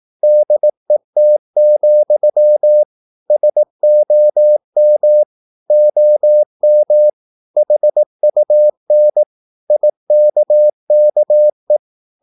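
Morse code, heard as a keyed tone, about 18 wpm, 600 Hz. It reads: DET, SOM OM HUN IKKE
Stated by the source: Slow (QRS) Morse practice at 18 wpm, CW med Gnister.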